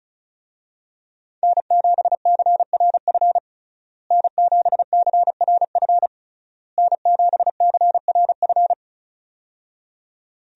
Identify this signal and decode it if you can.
Morse code, heard as a keyed tone, about 36 wpm, 700 Hz. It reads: N7CRF N7CRF N7CRF
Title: USA Call Signs - 5 Characters - Repeated 3x - 35wpm